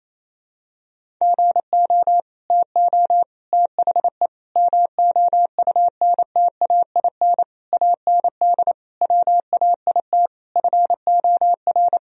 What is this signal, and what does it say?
Morse code, heard as a keyed tone, about 28 wpm, 700 Hz. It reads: GO TO THE MOUNTAIN AND WAIT FOR
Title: Sentences from Top 1000 Words - Review Part 1 - 28wpm